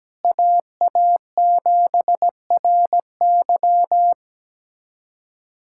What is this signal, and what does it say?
Morse code, heard as a keyed tone, about 17 wpm, 700 Hz. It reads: AA7RY
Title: USA Callsign DE Callsign - 5 Characters - 17wpm